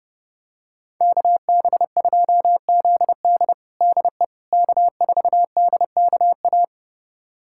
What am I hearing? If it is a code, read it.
Morse code, heard as a keyed tone, about 32 wpm, 700 Hz. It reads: KB2ZD DE K4DKA
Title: USA Callsign DE Callsign - 5 Characters - 30wpm